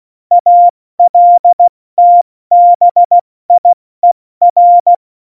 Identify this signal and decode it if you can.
Morse code, heard as a keyed tone, about 16 wpm, 700 Hz. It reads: ALTBIER
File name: Morsekurs / Witze / Witz 10.Dezember 2019, 70 F80